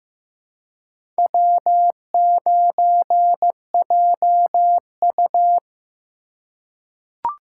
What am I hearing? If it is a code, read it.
Morse code, heard as a keyed tone, about 15 wpm, 700 Hz. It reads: W9JU
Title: USA Call Signs - 4 Characters - 4x Character Spacing - 15wpm